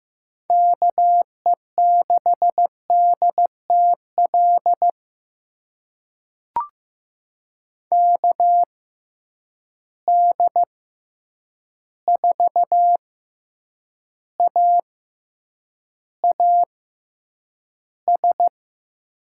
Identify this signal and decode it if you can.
Morse code, heard as a keyed tone, about 15 wpm, spaced as for 8 wpm, 700 Hz. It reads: KE6DTL KD4AAS
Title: USA Call Signs - 6 Characters - 6x Character Spacing - 15wpm